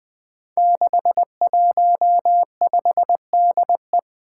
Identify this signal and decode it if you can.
Morse code, heard as a keyed tone, about 20 wpm, 700 Hz. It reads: 615DE
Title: Five Challenging Letter-Number Combinations 20wpm